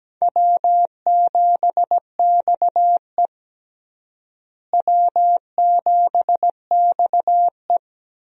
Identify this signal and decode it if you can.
Morse code, heard as a keyed tone, about 17 wpm, 700 Hz. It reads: W7XE W7XE
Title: USA Call Signs - 4 Characters - Repeated 3x - 17wpm